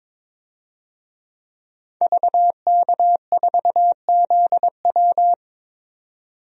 Morse code, heard as a keyed tone, about 22 wpm, 700 Hz. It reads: VK4ZW
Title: DXCC-100 Call Signs - 5 Characters - 22wpm